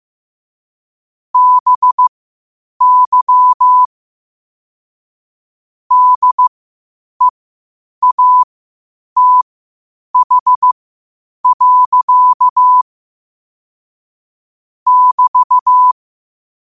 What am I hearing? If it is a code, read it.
Morse code, heard as a keyed tone, about 15 wpm, spaced as for 8 wpm, 1000 Hz. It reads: BY DEATH. <BT>
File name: qotdfiles / 8wpm